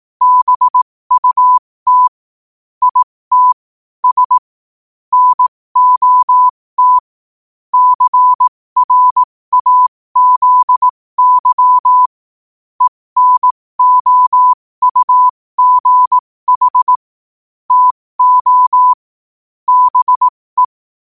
Morse code, heard as a keyed tone, about 18 wpm, spaced as for 15 wpm, 1000 Hz. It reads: BUT IT S NOT CRAZY ENOUGH TO BE
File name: qotdfiles / 15wpm